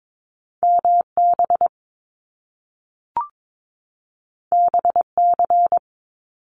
Morse code, heard as a keyed tone, about 22 wpm, 700 Hz. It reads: MB BC
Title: Canadian Provinces and Territories 22wpm